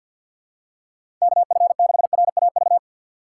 Morse code, heard as a keyed tone, about 53 wpm, 700 Hz. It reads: KF6RAV
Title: USA Call Signs - 6 Characters - 4x Character Spacing - 50wpm